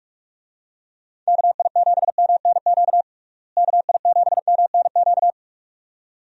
Morse code, heard as a keyed tone, about 46 wpm, 700 Hz. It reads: KI7MNQ KI7MNQ
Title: USA Call Signs - 6 Characters - Repeated 2x - 45wpm